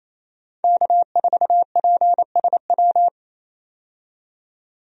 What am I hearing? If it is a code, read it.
Morse code, heard as a keyed tone, about 28 wpm, 700 Hz. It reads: K4PSW
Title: USA Call Signs - 5 Characters - 28wpm